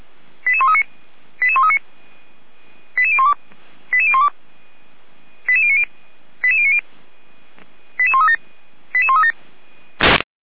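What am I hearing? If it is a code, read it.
Five-tone selective call (ZVEI), heard as quick runs of five tones, tones of about 70 ms. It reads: 88139 88139 88012 88012 88090 88090 88137 88137
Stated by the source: ZVEI five-tone selective call received on SDR, feuerwehr-orlamuende fire brigade.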